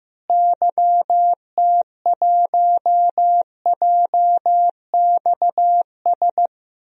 Morse code, heard as a keyed tone, about 15 wpm, 700 Hz. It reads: YT1JXS
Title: DXCC-100 Call Signs - 6 Characters - Repeated 2x - 15wpm